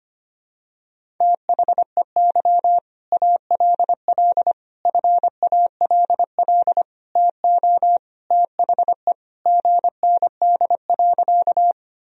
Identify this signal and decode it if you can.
Morse code, heard as a keyed tone, about 25 wpm, 700 Hz. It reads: THEY ALL FALL TO THE GND.
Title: Sentences from Top 500 Words and Abbreviations 25wpm